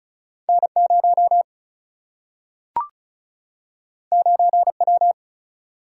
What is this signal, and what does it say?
Morse code, heard as a keyed tone, about 36 wpm, 700 Hz. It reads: N 0 9 W
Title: Single-Letter Single-Number 35wpm